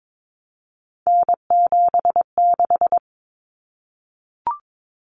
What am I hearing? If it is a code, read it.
Morse code, heard as a keyed tone, about 22 wpm, 700 Hz. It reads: N76
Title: Common Cut Numbers - 3-Digit 22wpm